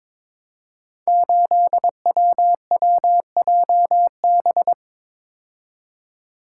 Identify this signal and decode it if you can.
Morse code, heard as a keyed tone, about 22 wpm, 700 Hz. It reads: 8WWJB
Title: Code Groups - 5 Letter:Numbers - 22wpm